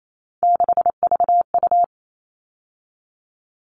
Morse code, heard as a keyed tone, about 28 wpm, 700 Hz. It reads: 6VU
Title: All Cut Numbers - 3-Digit 28wpm